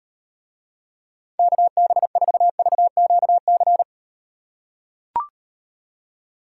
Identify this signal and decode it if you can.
Morse code, heard as a keyed tone, about 38 wpm, 700 Hz. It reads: KB4VQC